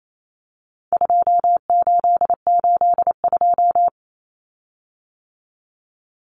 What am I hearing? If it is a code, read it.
Morse code, heard as a keyed tone, about 28 wpm, 700 Hz. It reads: 2882
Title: Common Cut Numbers - 4-Digit 28wpm